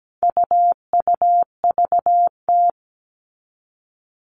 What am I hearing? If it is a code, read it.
Morse code, heard as a keyed tone, about 17 wpm, 700 Hz. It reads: UUVT
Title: All Cut Numbers - 4-Digit 17wpm